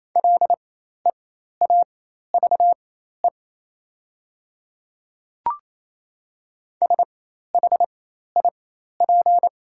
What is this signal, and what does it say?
Morse code, heard as a keyed tone, about 28 wpm, spaced as for 13 wpm, 700 Hz. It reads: LEAVE SHIP
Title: 4x Character Spacing - Top 500 Words - 28wpm